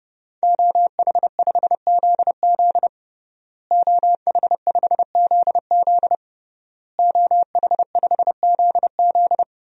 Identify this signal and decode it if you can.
Morse code, heard as a keyed tone, about 32 wpm, 700 Hz. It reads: OH5ZZ OH5ZZ OH5ZZ
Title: DXCC-100 Call Signs - 5 Characters - Repeated 3x - 30wpm